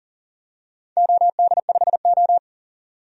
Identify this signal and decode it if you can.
Morse code, heard as a keyed tone, about 43 wpm, 700 Hz. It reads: OD5O
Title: DXCC-100 Call Signs - 4 Characters - 40wpm